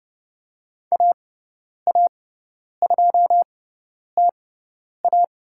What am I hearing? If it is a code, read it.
Morse code, heard as a keyed tone, about 30 wpm, spaced as for 10 wpm, 700 Hz. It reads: AA2TA
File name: Lesson 18g - Letter m - Call Signs 30wpm